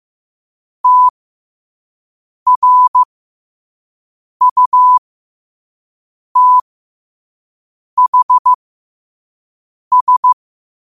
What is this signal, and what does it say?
Morse code, heard as a keyed tone, about 15 wpm, spaced as for 5 wpm, 1000 Hz. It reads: TRUTHS
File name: qotdfiles / 5wpm